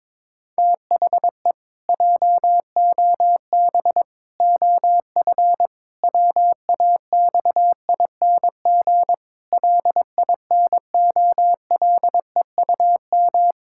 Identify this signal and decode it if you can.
Morse code, heard as a keyed tone, about 22 wpm, 700 Hz. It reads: THE JOB OF WAXING LINOLEUM
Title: Pangram Challenge v2 - 22wpm